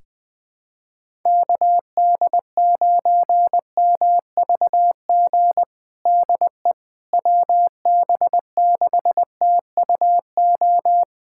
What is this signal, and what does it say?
Morse code, heard as a keyed tone, about 20 wpm, 700 Hz. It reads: KD9MVG DE WB6TUO
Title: USA Callsign DE Callsign 20wpm